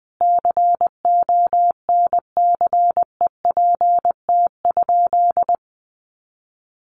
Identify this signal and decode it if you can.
Morse code, heard as a keyed tone, about 19 wpm, 700 Hz. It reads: CONCEPT?
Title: Sentences from Top 1100 Words 20wpm